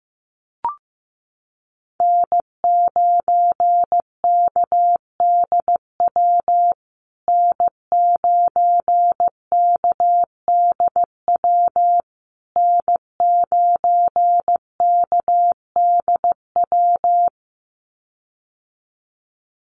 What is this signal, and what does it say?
Morse code, heard as a keyed tone, about 15 wpm, 700 Hz. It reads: N9KDW N9KDW N9KDW
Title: USA Call Signs Repeated 3x 15wpm